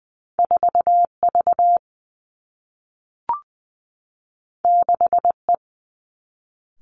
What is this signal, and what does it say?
Morse code, heard as a keyed tone, about 20 wpm, 700 Hz. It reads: 4 V 6 E
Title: Two Challenging Letter-Number Combinations 20wpm